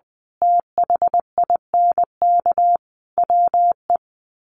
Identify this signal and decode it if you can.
Morse code, heard as a keyed tone, about 20 wpm, 700 Hz. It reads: THINK WE
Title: Sets of 5 Words 20wpm